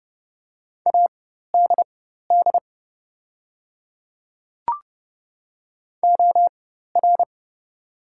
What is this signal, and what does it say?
Morse code, heard as a keyed tone, about 30 wpm, spaced as for 14 wpm, 690 Hz. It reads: ADD OR
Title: 4x Character Spacing - Top 500 Words - 30wpm